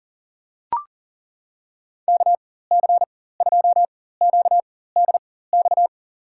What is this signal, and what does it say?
Morse code, heard as a keyed tone, about 38 wpm, spaced as for 18 wpm, 700 Hz. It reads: KC2QDX
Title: USA Call Signs - 6 Characters - 4x Character Spacing - 40wpm